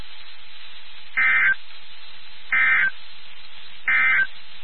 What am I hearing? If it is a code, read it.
Emergency Alert System (SAME) tones, heard as harsh buzzing bursts. It is heard from 1.1 to 4.4 s.
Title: NOAA Weather Radio WXK79